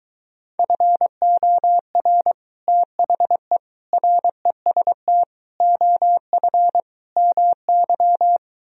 Morse code, heard as a keyed tone, about 23 wpm, 700 Hz. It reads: FOR THE REST OF MY